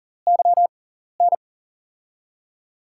Morse code, heard as a keyed tone, about 39 wpm, spaced as for 14 wpm, 700 Hz. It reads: YN
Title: USA Call Signs - 5 Characters - 6x Character Spacing - 40wpm